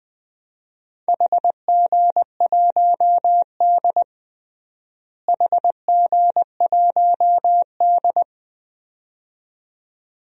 Morse code, heard as a keyed tone, about 20 wpm, 700 Hz. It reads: HG1D HG1D
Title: DXCC-100 Call Signs - 4 Characters - Repeated 2x - 20wpm